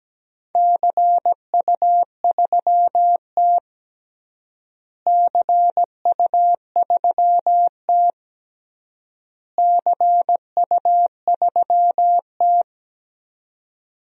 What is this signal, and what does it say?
Morse code, heard as a keyed tone, about 17 wpm, 700 Hz. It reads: CU3T CU3T CU3T
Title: DXCC-100 Call Signs - 4 Characters - Repeated 3x - 17wpm